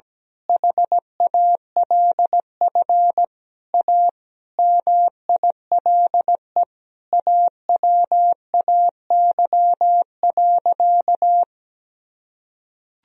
Morse code, heard as a keyed tone, about 17 wpm, 700 Hz. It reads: HALF A MILE AWAY.
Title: Sentences from Top 1000 Words - Review Part 2 - 17wpm